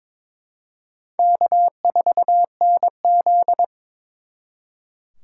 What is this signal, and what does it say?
Morse code, heard as a keyed tone, about 26 wpm, 700 Hz. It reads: K4NZ
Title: Speed Racing - USA 1x2 Call Signs 22wpm